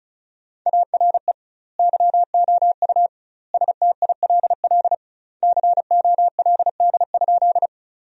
Morse code, heard as a keyed tone, about 34 wpm, 700 Hz. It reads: ARE YOU STILL COLD?
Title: 2x Word-Spacing - Sentences from Top 400 Words - 35wpm